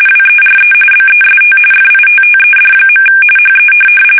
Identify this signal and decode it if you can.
RTTY, about 45.45 baud. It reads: WERE BEST FITTED. HE
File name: fsk850 / benjamin-g-lamme